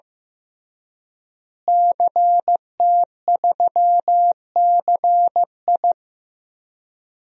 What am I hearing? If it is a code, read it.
Morse code, heard as a keyed tone, about 15 wpm, 700 Hz. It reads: CT3CI
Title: DXCC-100 Call Signs - 5 Characters - Repeated 3x - 15wpm